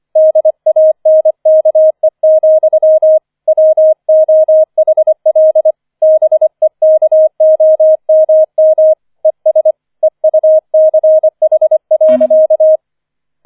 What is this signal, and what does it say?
Morse code, heard as a keyed tone, about 25 wpm, 630 Hz. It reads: DANKE, WOHL BEKOMM ES EUCH.
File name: 005 Fünfter Streich 25wpm